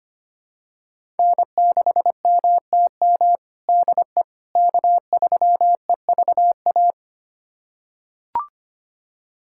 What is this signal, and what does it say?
Morse code, heard as a keyed tone, about 25 wpm, 700 Hz. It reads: N6MTM DE K3EVA
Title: USA Callsign DE Callsign - 5 Characters - 25wpm